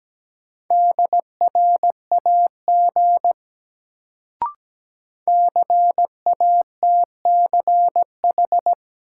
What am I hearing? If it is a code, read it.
Morse code, heard as a keyed tone, about 17 wpm, 700 Hz. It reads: DRAG CATCH
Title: Sentences from Top 1500 Words - New Word Drill - 17wpm